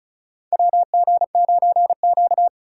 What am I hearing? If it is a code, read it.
Morse code, heard as a keyed tone, about 36 wpm, 700 Hz. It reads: WG9Q
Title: USA Call Signs - 4 Characters - 4x Character Spacing - 35wpm